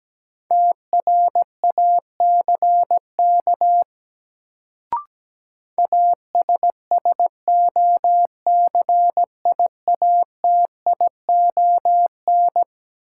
Morse code, heard as a keyed tone, about 17 wpm, 700 Hz. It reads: TRACK ASSOCIATION
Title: Sentences from Top 1100 Words - New Word Drill - 17wpm